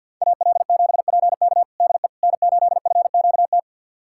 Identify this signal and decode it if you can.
Morse code, heard as a keyed tone, about 50 wpm, 700 Hz. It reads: AL7PK DE N8FQT
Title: USA Callsign DE Callsign - 5 Characters - 50wpm